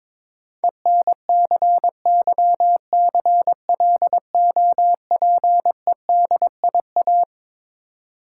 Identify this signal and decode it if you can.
Morse code, heard as a keyed tone, about 23 wpm, 690 Hz. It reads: ENCYCLOPEDIA
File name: Difficult Words with Hints - 22wpm